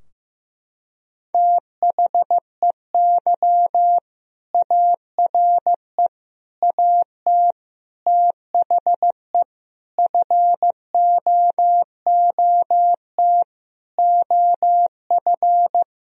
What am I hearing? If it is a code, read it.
Morse code, heard as a keyed tone, about 16 wpm, 700 Hz. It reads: THEY ARE AT THE FOOT OF
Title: Sentences from Top 1000 Words - Review Part 4 - 15wpm